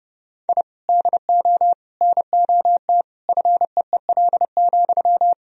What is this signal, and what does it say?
Morse code, heard as a keyed tone, about 30 wpm, 700 Hz. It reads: I DO NOT FEEL,
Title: Sentences from Top 1000 Words - Review Part 3 - 30wpm